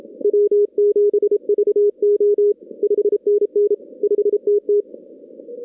Morse code, heard as a keyed tone, about 29 wpm, 410 Hz. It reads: W7VO 5NN 5TT
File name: 2016 arrl dx cw / 2016-ARRL-DX-CW-K5TR- - 0-46